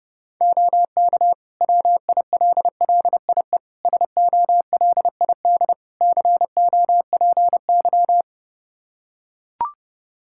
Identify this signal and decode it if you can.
Morse code, heard as a keyed tone, about 32 wpm, 700 Hz. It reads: OK WILLIE SOLID COPY
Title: Practice QSO - Exchange 3 - More Info - 30wpm